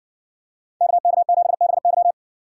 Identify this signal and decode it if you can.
Morse code, heard as a keyed tone, about 60 wpm, 700 Hz. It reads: KK7BY